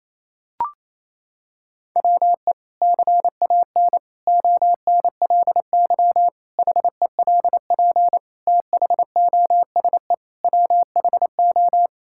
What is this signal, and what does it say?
Morse code, heard as a keyed tone, about 27 wpm, 700 Hz. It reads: WE CAN ONLY HELP THOSE WHO
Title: Sentences from Top 1000 Words - Review Part 5 - 28wpm